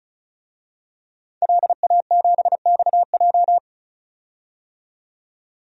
Morse code, heard as a keyed tone, about 34 wpm, 700 Hz. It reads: LA7XJ
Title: DXCC-100 Call Signs - 5 Characters - 35wpm